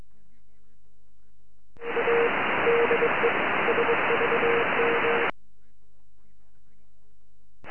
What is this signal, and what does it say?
Morse code, heard as a keyed tone, about 22 wpm, 470 Hz, with radio noise. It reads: A DE SVM
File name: sv1qt 4m